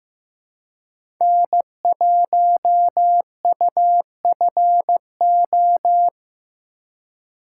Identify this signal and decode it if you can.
Morse code, heard as a keyed tone, about 15 wpm, 700 Hz. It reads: N1UFO
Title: USA Call Signs - 5 Characters - 4x Character Spacing - 15wpm